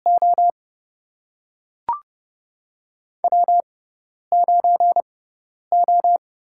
Morse code, heard as a keyed tone, about 30 wpm, spaced as for 10 wpm, 700 Hz. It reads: O W9O
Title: USA Call Signs - 5 Characters - 6x Character Spacing - 30wpm